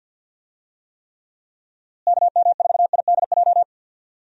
Morse code, heard as a keyed tone, about 50 wpm, 700 Hz. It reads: KM4IDJ
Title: USA Call Signs - 6 Characters - Repeated 3x - 50wpm